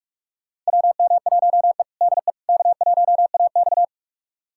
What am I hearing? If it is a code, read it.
Morse code, heard as a keyed tone, about 42 wpm, 700 Hz. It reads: WM1E DE K1AX
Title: USA Callsign DE Callsign - 4 Characters - 45wpm